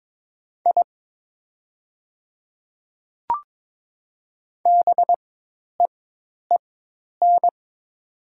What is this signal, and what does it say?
Morse code, heard as a keyed tone, about 23 wpm, spaced as for 10 wpm, 700 Hz. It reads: I BEEN
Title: 4x Character Spacing - Top 100 Words - 22wpm